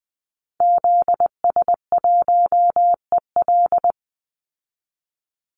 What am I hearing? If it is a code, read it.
Morse code, heard as a keyed tone, about 20 wpm, 700 Hz. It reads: ZS1EL
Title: Call Signs Repeated 2x - 20wpm